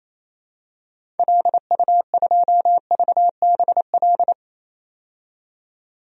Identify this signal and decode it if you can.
Morse code, heard as a keyed tone, about 28 wpm, 700 Hz. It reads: LU2VBL
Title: DXCC-100 Call Signs - 6 Characters - 28wpm